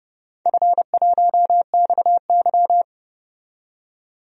Morse code, heard as a keyed tone, about 30 wpm, 700 Hz. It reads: F1XY
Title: DXCC-100 Call Signs - 4 Characters - 30wpm